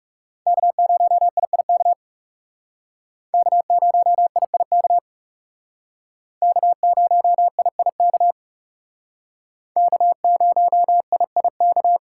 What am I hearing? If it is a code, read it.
Morse code, heard as a keyed tone, about 38 wpm, 700 Hz. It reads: K0IIK K0IIK K0IIK K0IIK